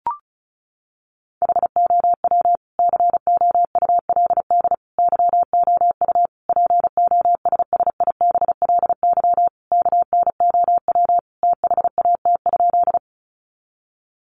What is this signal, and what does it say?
Morse code, heard as a keyed tone, about 34 wpm, 700 Hz. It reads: HOW COULD YOU POSSIBLY KNOW THAT?
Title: Sentences from Top 1300 Words 35wpm